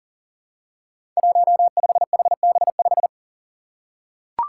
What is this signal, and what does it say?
Morse code, heard as a keyed tone, about 39 wpm, 700 Hz. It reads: 15HB5
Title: Five Challenging Letter-Number Combinations 40wpm